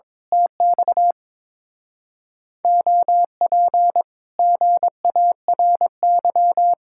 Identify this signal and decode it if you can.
Morse code, heard as a keyed tone, about 22 wpm, 700 Hz. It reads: TX OP GARY
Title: Speed-Racing Common QSO Phrases - 22wpm